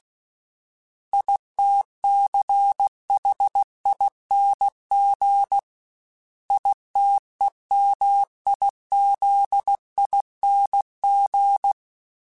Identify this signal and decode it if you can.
Morse code, heard as a keyed tone, about 16 wpm, 800 Hz. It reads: ITCHING ITEMIZING
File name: Special INGs / Special INGs-mp3s / Sngl ing @15